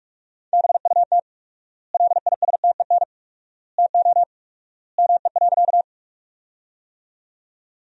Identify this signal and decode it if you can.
Morse code, heard as a keyed tone, about 46 wpm, 700 Hz. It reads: BUT LISTEN TO ME.
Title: 4x Word-Spacing - Sentences from Top 400 Words - 45wpm